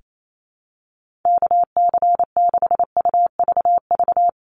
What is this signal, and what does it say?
Morse code, heard as a keyed tone, about 28 wpm, 700 Hz. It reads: KC6UVV
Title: USA Call Signs 28wpm